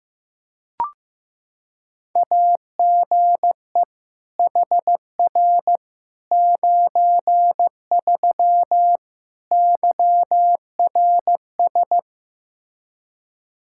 Morse code, heard as a keyed tone, about 15 wpm, 700 Hz. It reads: AGE HR 93 YRS
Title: Practice QSO - Exchange 3 - More Info - 15wpm